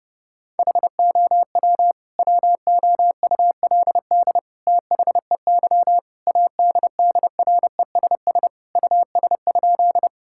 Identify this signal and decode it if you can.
Morse code, heard as a keyed tone, about 28 wpm, 700 Hz. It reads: HOW WOULD THEY ADDRESS US?
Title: Sentences from Top 1000 Words - Review Part 6 - 30wpm